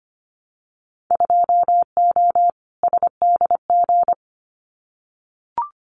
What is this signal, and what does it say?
Morse code, heard as a keyed tone, about 25 wpm, 700 Hz. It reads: 2O SDG
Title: Winter Field Day Exchanges 25wpm